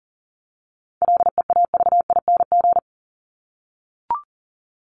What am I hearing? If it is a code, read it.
Morse code, heard as a keyed tone, about 40 wpm, 700 Hz. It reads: LEAVING
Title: ING Suffix - Top 100 Words with Base Words 40wpm